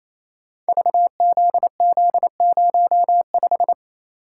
Morse code, heard as a keyed tone, about 29 wpm, 700 Hz. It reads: VZZ05
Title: Code Groups - 5 Letter:Numbers - 28wpm